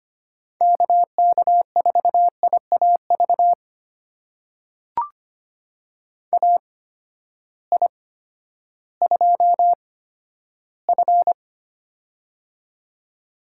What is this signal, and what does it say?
Morse code, heard as a keyed tone, about 25 wpm, spaced as for 12 wpm, 700 Hz. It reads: KK4IAV AI2F
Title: USA Call Signs - 8x Character Spacing - 25wpm